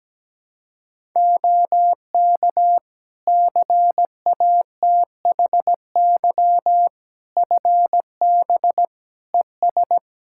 Morse code, heard as a keyed tone, about 17 wpm, 700 Hz. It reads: OK CATHY FB ES
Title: K1USN SST QSO Practice 17wpm